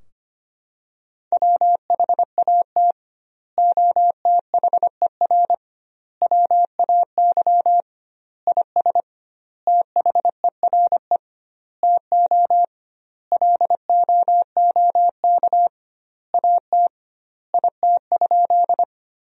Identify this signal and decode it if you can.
Morse code, heard as a keyed tone, about 25 wpm, 700 Hz. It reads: WHAT OTHER WAY IS THERE TO LOOK AT IT?
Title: 2x Word-Spacing - Sentences from Top 100 Words - 25wpm